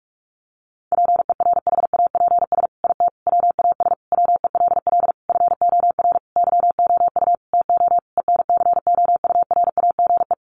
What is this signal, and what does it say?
Morse code, heard as a keyed tone, about 44 wpm, 700 Hz. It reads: PERHAPS IT WAS WELL FOR YOU TO ENCOURAGE
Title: Sentences from Top 1400 Words 45wpm